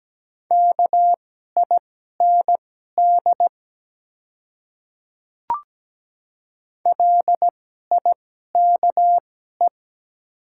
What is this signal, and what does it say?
Morse code, heard as a keyed tone, about 17 wpm, spaced as for 12 wpm, 700 Hz. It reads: KIND LIKE
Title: 2x Character Spacing - Top 300 Words - 17wpm